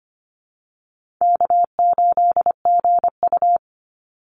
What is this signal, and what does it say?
Morse code, heard as a keyed tone, about 25 wpm, 700 Hz. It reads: K8GU
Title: Easy Call Signs Repeated 3x - 25wpm